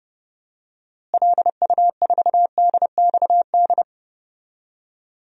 Morse code, heard as a keyed tone, about 31 wpm, 700 Hz. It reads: LU4DXD
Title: DXCC-100 Call Signs - 6 Characters - Repeated 3x - 30wpm